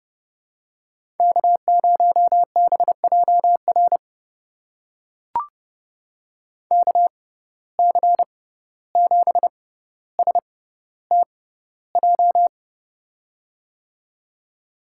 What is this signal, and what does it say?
Morse code, heard as a keyed tone, about 32 wpm, spaced as for 15 wpm, 700 Hz. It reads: K0BJR KC7STJ
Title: USA Call Signs - 6x Character Spacing - 30wpm